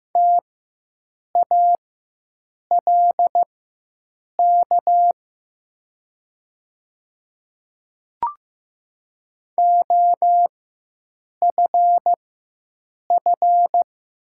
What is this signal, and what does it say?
Morse code, heard as a keyed tone, about 15 wpm, spaced as for 7 wpm, 700 Hz. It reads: TALK OFF
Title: 4x Character Spacing - Top 500 Words - 15wpm